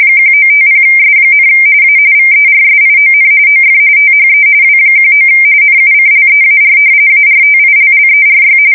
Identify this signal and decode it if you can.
RTTY, about 45.45 baud. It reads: 1950S, A VERSION OF THE W19 NUCLEAR ARTILLERY
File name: fsk170 / battleshipnj